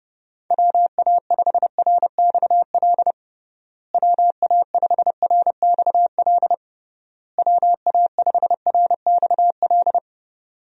Morse code, heard as a keyed tone, about 30 wpm, 700 Hz. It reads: WA5RXL WA5RXL WA5RXL
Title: USA Call Signs - 6 Characters - Repeated 3x - 30wpm